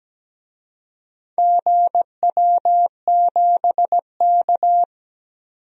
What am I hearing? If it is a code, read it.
Morse code, heard as a keyed tone, about 17 wpm, 700 Hz. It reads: GW7K
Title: DXCC-100 Call Signs - 4 Characters - 17wpm